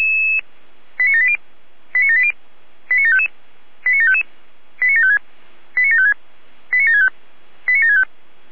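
Five-tone selective call (ZVEI), heard as quick runs of five tones, tones of about 70 ms. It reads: 89780 89780 89755 89755 89756 89756 89765 89765
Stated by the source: ZVEI five-tone selective call received on an SDR.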